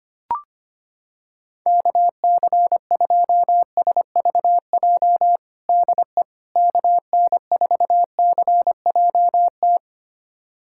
Morse code, heard as a keyed tone, about 25 wpm, 700 Hz. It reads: KC2SVJ DE KN4CJT
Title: USA Callsign DE Callsign - 6 Characters - 25wpm